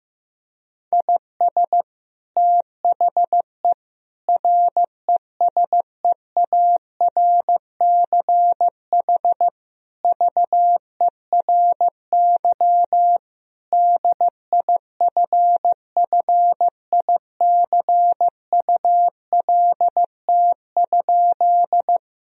Morse code, heard as a keyed tone, about 16 wpm, 700 Hz. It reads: IS THE RESEARCH VERY DIFFICULT?
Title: Sentences from Top 1000 Words - Review Part 4 - 15wpm